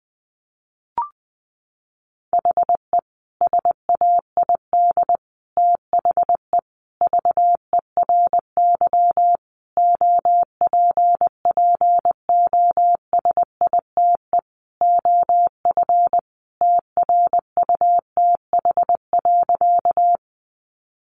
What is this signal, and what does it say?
Morse code, heard as a keyed tone, about 20 wpm, 700 Hz. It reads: HE SAID THE VERY OPPOSITE OF TRUTH.
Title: Sentences from Top 1300 Words 20wpm